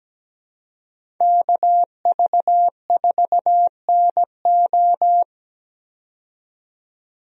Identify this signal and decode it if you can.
Morse code, heard as a keyed tone, about 17 wpm, 700 Hz. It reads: KV4NO
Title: USA Call Signs - 5 Characters - 17wpm